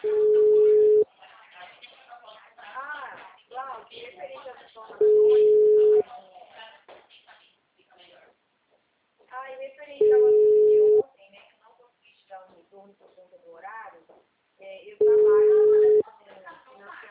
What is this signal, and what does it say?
Ringback tone (Europe, 425 Hz), with speech.
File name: out- -2019- - - .29